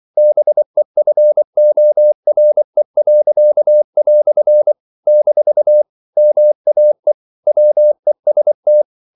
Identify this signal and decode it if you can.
Morse code, heard as a keyed tone, about 24 wpm, 600 Hz. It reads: BEFORE." - MAE WEST
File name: CW / Practice / EnglishQuotations0324